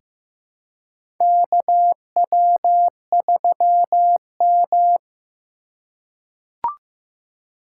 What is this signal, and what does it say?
Morse code, heard as a keyed tone, about 15 wpm, 700 Hz. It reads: KW3M
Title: USA Call Signs - 2x Character Spacing - 15wpm